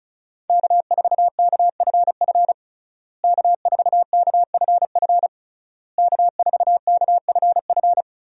Morse code, heard as a keyed tone, about 36 wpm, 700 Hz. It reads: K4KFF K4KFF K4KFF
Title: USA Call Signs - 5 Characters - Repeated 3x - 35wpm